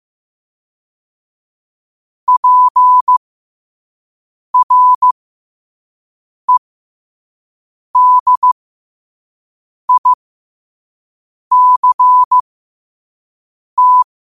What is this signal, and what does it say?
Morse code, heard as a keyed tone, about 15 wpm, spaced as for 5 wpm, 1000 Hz. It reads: PREDICT
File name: qotdfiles / 5wpm